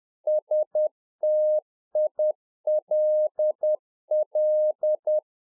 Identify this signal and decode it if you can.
Morse code, heard as a keyed tone, about 10 wpm, 610 Hz. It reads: STILL
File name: 2011-04-mp3 / RICK JJ0MPI 10WPM